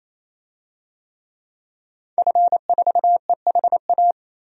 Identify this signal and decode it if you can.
Morse code, heard as a keyed tone, about 27 wpm, 700 Hz. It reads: F4EHA
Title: DXCC-100 Call Signs - 5 Characters - 28wpm